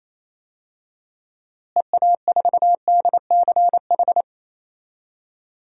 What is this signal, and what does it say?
Morse code, heard as a keyed tone, about 28 wpm, 700 Hz. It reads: EA4DCH
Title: DXCC-100 Call Signs - 6 Characters - Repeated 2x - 28wpm